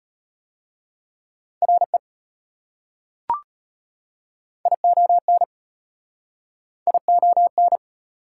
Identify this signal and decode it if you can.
Morse code, heard as a keyed tone, about 36 wpm, 700 Hz. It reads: RE ION ION